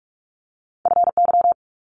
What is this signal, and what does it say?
Morse code, heard as a keyed tone, about 46 wpm, 700 Hz. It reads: FY
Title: Multi-letter Phonemes 45wpm